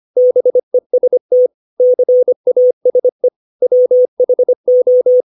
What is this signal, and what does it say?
Morse code, heard as a keyed tone, about 25 wpm, 500 Hz. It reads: BEST CASE WHO
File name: 632 words / 632 Words 25WPM